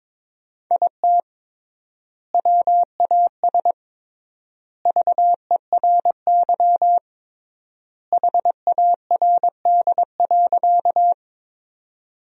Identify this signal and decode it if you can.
Morse code, heard as a keyed tone, about 22 wpm, 700 Hz. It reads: IT WAS VERY HARD.
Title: 3x Word-Spacing - Sentences from Top 300 Words - 22wpm